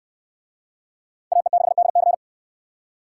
Morse code, heard as a keyed tone, about 69 wpm, 700 Hz. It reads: AE6DC